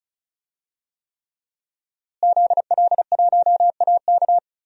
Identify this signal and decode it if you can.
Morse code, heard as a keyed tone, about 34 wpm, 700 Hz. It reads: ZL1AK
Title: DXCC-100 Call Signs - 5 Characters - Repeated 3x - 35wpm